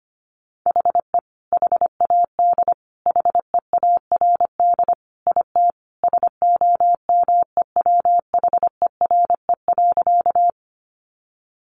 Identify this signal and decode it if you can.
Morse code, heard as a keyed tone, about 25 wpm, 700 Hz. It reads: HE HAD HEARD IT SOMEWHERE.
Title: Sentences from Top 1400 Words 25wpm